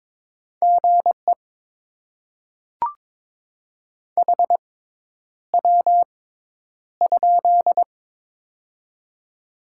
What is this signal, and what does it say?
Morse code, heard as a keyed tone, about 23 wpm, spaced as for 10 wpm, 690 Hz. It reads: GE HW?
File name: Top 20 QSO Elements - 6x Character Spacing - 22wpm